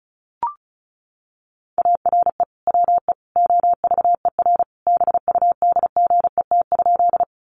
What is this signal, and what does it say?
Morse code, heard as a keyed tone, about 34 wpm, 700 Hz. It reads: ARE WE OVER BUDGET?
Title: Sentences from Top 1200 Words 35wpm